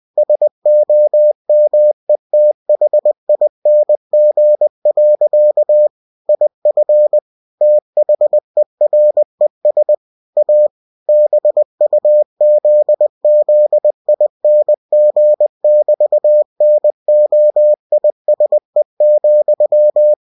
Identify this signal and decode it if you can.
Morse code, heard as a keyed tone, about 20 wpm, 600 Hz. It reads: SOMETHING. IF THERES A BUZZING<BT>NOISE,
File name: cw-books / winnie-the-pooh / winnie-20wpm-page-0002